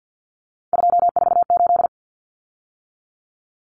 Morse code, heard as a keyed tone, about 50 wpm, 700 Hz. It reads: 248
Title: Numbers - 3-Digit 50wpm